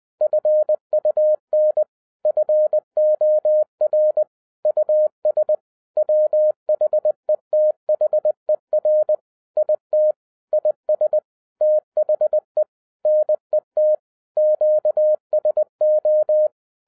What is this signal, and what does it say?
Morse code, heard as a keyed tone, about 20 wpm, 610 Hz. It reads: FUN FOR US WHETHER IT IS THE NET QSO
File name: 2012-01-mp3 / n16-THROUGH EIGHT MONTHS-20wpm